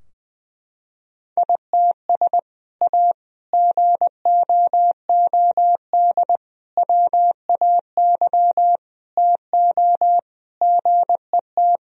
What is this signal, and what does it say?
Morse code, heard as a keyed tone, about 20 wpm, 700 Hz. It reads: ITS A GOOD WAY TO GET
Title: Sentences from Top 1000 Words - Review Part 3 - 20wpm